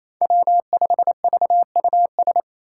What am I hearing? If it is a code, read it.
Morse code, heard as a keyed tone, about 28 wpm, 700 Hz. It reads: W5VUS
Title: Five Challenging Letter-Number Combinations 28wpm